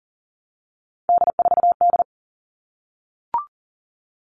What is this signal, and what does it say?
Morse code, heard as a keyed tone, about 40 wpm, 700 Hz. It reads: D4D
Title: All Cut Numbers - 3-Digit 40wpm